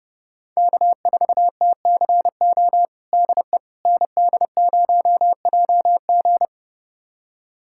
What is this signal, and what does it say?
Morse code, heard as a keyed tone, about 30 wpm, 700 Hz. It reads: K4TCO DE ND0JG
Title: USA Callsign DE Callsign - 5 Characters - 30wpm